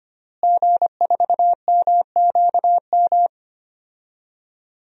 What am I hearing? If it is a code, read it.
Morse code, heard as a keyed tone, about 25 wpm, 700 Hz. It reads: G4MQM
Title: DXCC-100 Call Signs - 5 Characters - Repeated 3x - 25wpm